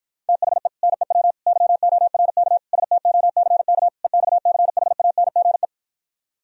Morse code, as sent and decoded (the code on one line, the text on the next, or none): - .... / -. .-- / -.-- --- .-. -.- / ... - --- -.-. -.- / . -..- -.-. .... .- -. --. .
THE NEW YORK STOCK EXCHANGE